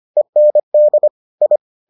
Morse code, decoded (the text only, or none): END I